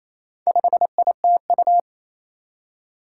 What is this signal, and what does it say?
Morse code, heard as a keyed tone, about 28 wpm, 700 Hz. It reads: 5ITU
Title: Four Challenging Letter-Number Combinations 28wpm